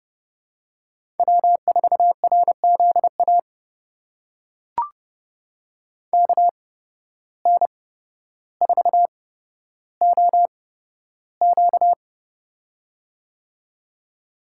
Morse code, heard as a keyed tone, about 30 wpm, spaced as for 13 wpm, 700 Hz. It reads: W4RZA KN4OQ
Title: USA Call Signs - 5 Characters - 8x Character Spacing - 30wpm